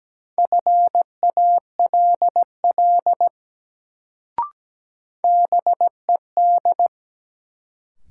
Morse code, heard as a keyed tone, about 17 wpm, 700 Hz. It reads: FALL BED
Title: Sentences from Top 500 Words - New Word Drill - 17wpm